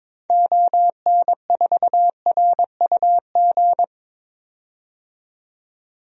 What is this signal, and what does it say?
Morse code, heard as a keyed tone, about 22 wpm, 700 Hz. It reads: ON4RUG
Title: DXCC-100 Call Signs - 6 Characters - 22wpm